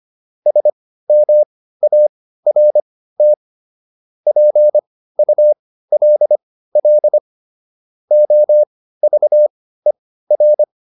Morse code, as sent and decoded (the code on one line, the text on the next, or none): ... -- .- .-. - / .--. ..- .-.. .-.. / --- ...- . .-.
SMART PULL OVER